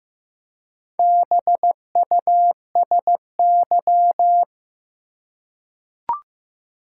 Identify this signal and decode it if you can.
Morse code, heard as a keyed tone, about 15 wpm, 700 Hz. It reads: BUSY
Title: Four Letter Words - 15wpm